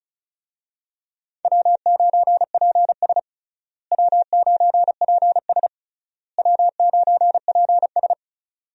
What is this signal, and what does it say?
Morse code, heard as a keyed tone, about 36 wpm, 700 Hz. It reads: W9PS W9PS W9PS
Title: USA Call Signs - 4 Characters - Repeated 3x - 35wpm